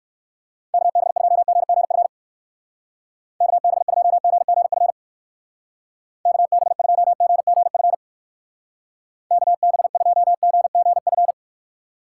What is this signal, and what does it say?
Morse code, heard as a keyed tone, about 65 wpm, 700 Hz. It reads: KB2GGF KB2GGF KB2GGF KB2GGF